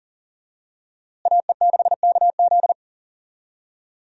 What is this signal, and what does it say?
Morse code, heard as a keyed tone, about 40 wpm, 700 Hz. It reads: AE6KZ
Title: USA Call Signs - 5 Characters - Repeated 2x - 40wpm